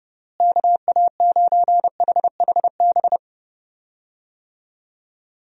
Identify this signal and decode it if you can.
Morse code, heard as a keyed tone, about 30 wpm, 700 Hz. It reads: KA9HHB
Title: USA Call Signs - 6 Characters - Repeated 3x - 30wpm